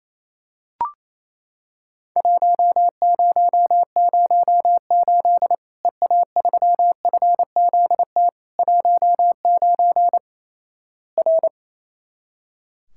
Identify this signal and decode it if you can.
Morse code, heard as a keyed tone, about 28 wpm, 700 Hz. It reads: 1008 EA3FZT 19 R
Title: Worked All Europe DX Contest - QTC Exchange - 28wpm